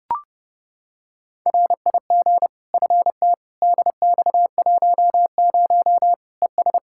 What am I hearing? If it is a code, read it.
Morse code, heard as a keyed tone, about 30 wpm, 700 Hz. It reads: RIG FT DX10 ES
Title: Practice QSO - Exchange 2 - Info - 30wpm